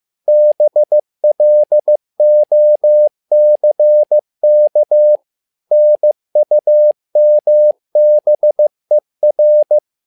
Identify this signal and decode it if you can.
Morse code, heard as a keyed tone, about 15 wpm, 600 Hz. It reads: BLOCK NUMBER
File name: learning center / tower trouble / tower trouble 15wpm